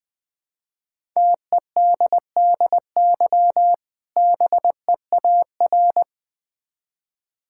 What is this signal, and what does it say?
Morse code, heard as a keyed tone, about 21 wpm, 700 Hz. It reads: TEDDY BEAR
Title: Difficult Words with Hints - 20wpm